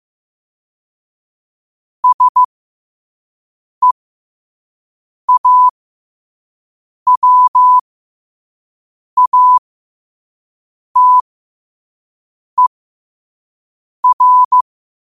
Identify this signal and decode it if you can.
Morse code, heard as a keyed tone, about 15 wpm, spaced as for 5 wpm, 1000 Hz. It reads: SEAWATER
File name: qotdfiles / 5wpm